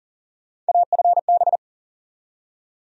Morse code, heard as a keyed tone, about 40 wpm, 700 Hz. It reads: AFB
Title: USA Call Sign Postfixes - 3 Letters 40wpm